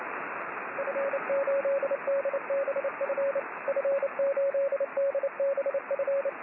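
Morse code, heard as a keyed tone, about 29 wpm, 550 Hz, with radio noise. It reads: F8DBF F8DBF
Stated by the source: Morse signal on the 6 m band.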